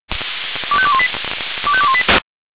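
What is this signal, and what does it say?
Five-tone selective call (ZVEI), heard as quick runs of five tones, tones of about 70 ms. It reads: 25218 25218
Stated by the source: ZVEI five-tone selective call received on an SDR.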